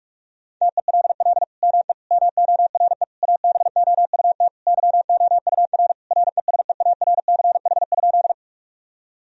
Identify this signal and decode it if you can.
Morse code, heard as a keyed tone, about 41 wpm, 700 Hz. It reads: TELL ME MORE ABOUT YOUR RESEARCH?
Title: Sentences from Top 1000 Words - Review Part 4 - 45wpm